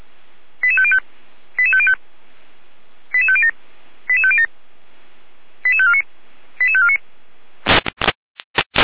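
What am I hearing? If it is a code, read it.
Five-tone selective call (ZVEI), heard as quick runs of five tones, tones of about 70 ms. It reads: 88505 88505 88507 88507 88549 88549
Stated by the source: ZVEI five-tone selective call received with an SDR.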